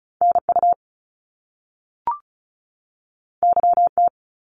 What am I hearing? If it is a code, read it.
Morse code, heard as a keyed tone, about 36 wpm, 700 Hz. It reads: NU YT
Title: Canadian Provinces and Territories 35wpm